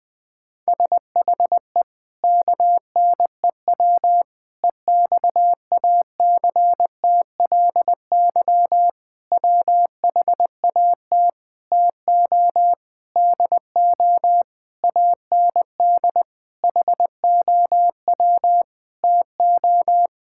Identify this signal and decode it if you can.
Morse code, heard as a keyed tone, about 20 wpm, 700 Hz. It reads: SHE KNEW EXACTLY WHAT TO DO AND HOW TO
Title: Sentences with 8-Word Phrases - 20wpm